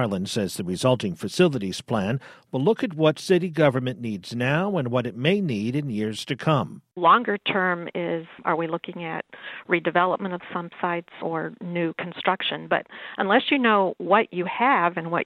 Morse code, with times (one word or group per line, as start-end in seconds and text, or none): none